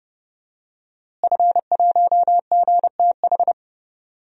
Morse code, heard as a keyed tone, about 30 wpm, 700 Hz. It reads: F1GTH
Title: DXCC-100 Call Signs - 5 Characters - Repeated 3x - 30wpm